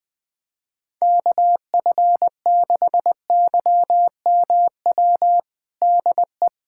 Morse code, heard as a keyed tone, about 20 wpm, 700 Hz. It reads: KF6YMW DE
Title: USA Callsign DE Callsign 20wpm